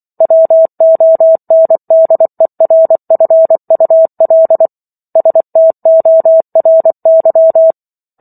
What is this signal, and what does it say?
Morse code, heard as a keyed tone, about 24 wpm, 650 Hz. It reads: WONDERFUL STORY